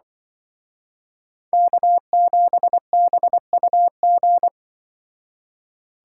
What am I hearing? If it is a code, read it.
Morse code, heard as a keyed tone, about 24 wpm, 700 Hz. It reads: K7BUG
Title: Speed Racing - USA 1x3 Call Signs 20wpm